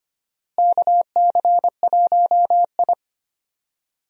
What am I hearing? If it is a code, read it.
Morse code, heard as a keyed tone, about 25 wpm, 700 Hz. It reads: KC1I
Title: USA Call Signs - 4 Characters - Repeated 3x - 25wpm